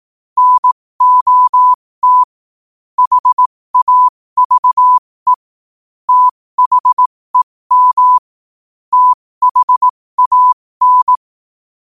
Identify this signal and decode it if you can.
Morse code, heard as a keyed tone, about 18 wpm, spaced as for 15 wpm, 1000 Hz. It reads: NOT HAVE THEM THAN
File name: qotdfiles / 15wpm